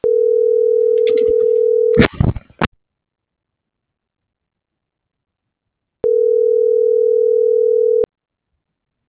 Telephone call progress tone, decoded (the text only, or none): ringback tone